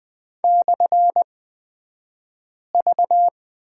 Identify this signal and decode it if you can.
Morse code, heard as a keyed tone, about 20 wpm, spaced as for 5 wpm, 700 Hz. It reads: /V
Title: 254 Lesson 41d - Period - Two Characters 20wpm